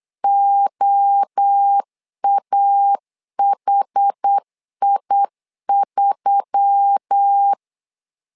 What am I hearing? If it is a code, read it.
Morse code, heard as a keyed tone, about 8 wpm, 800 Hz. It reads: OAHI3
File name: LEZIONI / 4 Q1S3 8wpm 3037